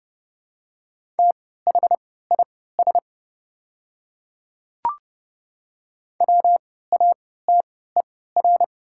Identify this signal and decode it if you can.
Morse code, heard as a keyed tone, about 30 wpm, spaced as for 17 wpm, 700 Hz. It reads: THIS WATER
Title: 3x Character Spacing - Top 100 Words - 30wpm